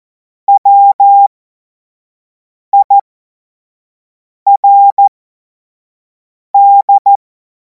Morse code, heard as a keyed tone, about 14 wpm, spaced as for 5 wpm, 800 Hz. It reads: WIRD
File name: Morsekurs / Witze / Witz 02. Juni 2026 25 F70